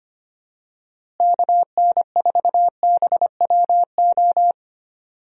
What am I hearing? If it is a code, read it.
Morse code, heard as a keyed tone, about 25 wpm, 700 Hz. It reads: KN4BWO
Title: USA Call Signs - 6 Characters - Repeated 2x - 25wpm